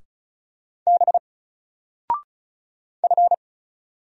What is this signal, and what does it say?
Morse code, heard as a keyed tone, about 36 wpm, 700 Hz. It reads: B F
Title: Single Character (Letter:Number:Punctuation:Prosign) Alt 35wpm